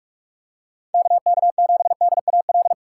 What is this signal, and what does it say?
Morse code, heard as a keyed tone, about 44 wpm, 700 Hz. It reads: KK7DAL